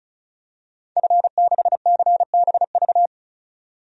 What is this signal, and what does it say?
Morse code, heard as a keyed tone, about 35 wpm, 700 Hz. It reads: F6CBV
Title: DXCC-100 Call Signs - 5 Characters - 35wpm